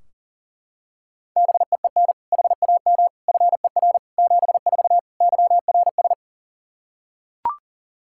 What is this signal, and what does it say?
Morse code, heard as a keyed tone, about 40 wpm, 700 Hz. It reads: BEEN HAM FER 74 YRS
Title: Practice QSO - Exchange 3 - More Info - 40wpm